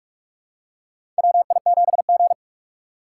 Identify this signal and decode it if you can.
Morse code, heard as a keyed tone, about 45 wpm, 700 Hz. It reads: WI7G
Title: USA Call Signs - 4 Characters - 4x Character Spacing - 45wpm